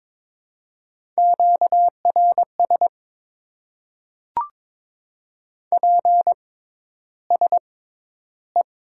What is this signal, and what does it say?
Morse code, heard as a keyed tone, about 22 wpm, spaced as for 11 wpm, 700 Hz. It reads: QRS PSE
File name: Top 90 QSO Elements - 6x Character Spacing - 22wpm